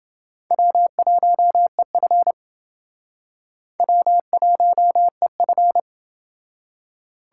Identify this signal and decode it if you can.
Morse code, heard as a keyed tone, about 28 wpm, 700 Hz. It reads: W1EF W1EF